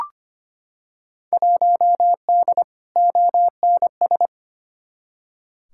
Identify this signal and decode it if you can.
Morse code, heard as a keyed tone, about 25 wpm, 700 Hz. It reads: 1D ONS
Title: Field Day Exchanges - 25wpm